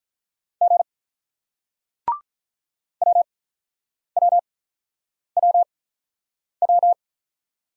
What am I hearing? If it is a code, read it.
Morse code, heard as a keyed tone, about 48 wpm, spaced as for 9 wpm, 700 Hz. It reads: G WWWW